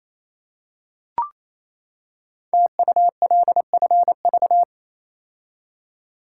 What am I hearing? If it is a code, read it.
Morse code, heard as a keyed tone, about 27 wpm, 700 Hz. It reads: TULFV
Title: Code Groups - 5 Letters - 28wpm